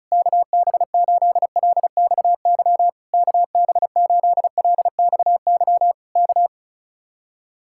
Morse code, heard as a keyed tone, about 32 wpm, 700 Hz. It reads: KB8LXY KB8LXY K
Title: Speed-Racing Common QSO Phrases - 35wpm